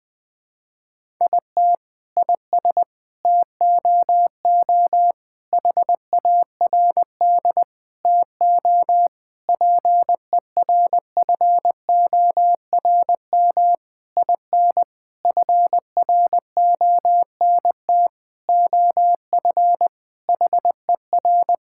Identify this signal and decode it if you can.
Morse code, heard as a keyed tone, about 20 wpm, 700 Hz. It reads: IT IS TOO HARD TO PERFORM IN FRONT OF HER
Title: Sentences from Top 1000 Words - Review Part 5 - 20wpm